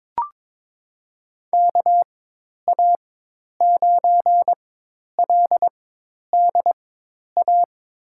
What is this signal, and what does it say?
Morse code, heard as a keyed tone, about 22 wpm, spaced as for 10 wpm, 700 Hz. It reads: KA9LDA
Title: USA Call Signs - 6 Characters - 4x Character Spacing - 22wpm